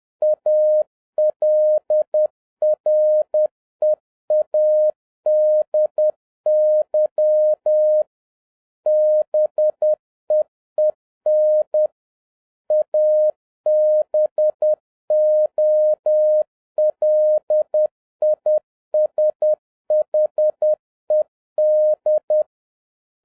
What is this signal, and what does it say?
Morse code, heard as a keyed tone, about 10 wpm, 610 Hz. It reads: ALREADY BEEN ABOLISHED